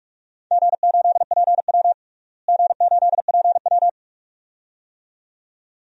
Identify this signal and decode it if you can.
Morse code, heard as a keyed tone, about 42 wpm, 700 Hz. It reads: G8PW G8PW
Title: DXCC-100 Call Signs - 4 Characters - Repeated 2x - 45wpm